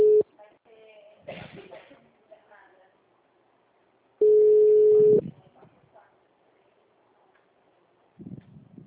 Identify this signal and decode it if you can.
Ringback tone (Europe, 425 Hz), with speech.